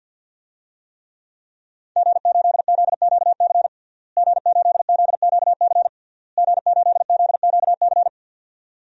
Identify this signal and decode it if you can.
Morse code, heard as a keyed tone, about 51 wpm, 700 Hz. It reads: G8ZQC G8ZQC G8ZQC
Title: DXCC-100 Call Signs - 5 Characters - Repeated 3x - 50wpm